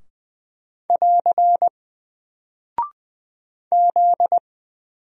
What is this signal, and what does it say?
Morse code, heard as a keyed tone, about 20 wpm, 700 Hz. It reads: <AR> Z
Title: Single Character (Letter:Number:Punctuation:Prosign) Alt 20wpm